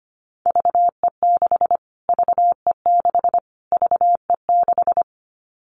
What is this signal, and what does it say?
Morse code, heard as a keyed tone, about 25 wpm, 700 Hz. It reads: VE6 VE6 VE6
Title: Callsign Prefixes for Top 25 DXCC - Repeated 3x - 25wpm